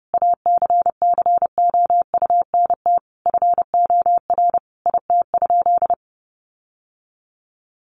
Morse code, heard as a keyed tone, about 30 wpm, 700 Hz. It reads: ACCOUNT FOR IT?
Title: Sentences from Top 900 Words 30wpm